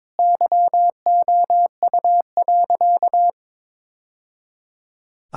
Morse code, heard as a keyed tone, about 21 wpm, 700 Hz. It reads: YOU.
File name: practice / 22wpm / Sentences with 2-Word Phrases - 22wpm